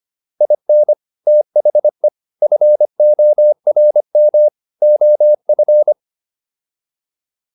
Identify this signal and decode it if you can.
Morse code, heard as a keyed tone, about 25 wpm, 600 Hz. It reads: IN THE FORM OF